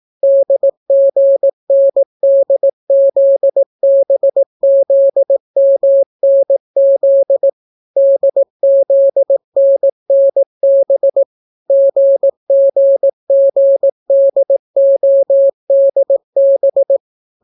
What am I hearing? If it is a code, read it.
Morse code, heard as a keyed tone, about 18 wpm, 550 Hz. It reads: DGNDZBZMNZ DZNNB GGGDODB